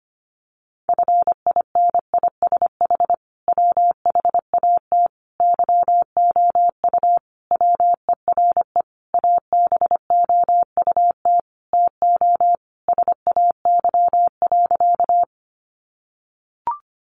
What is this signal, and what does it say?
Morse code, heard as a keyed tone, about 25 wpm, 700 Hz. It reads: FINISH WHAT YOU WERE ABOUT TO SAY.
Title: Sentences from Top 1200 Words 25wpm